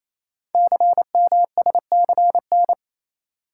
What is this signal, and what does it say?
Morse code, heard as a keyed tone, about 28 wpm, 700 Hz. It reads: CMSCN
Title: Code Groups - 5 Letters - 28wpm